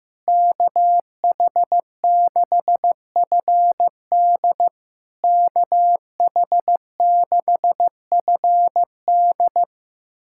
Morse code, heard as a keyed tone, about 15 wpm, 700 Hz. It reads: KH6FD KH6FD
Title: SOTA QSO Practice 15wpm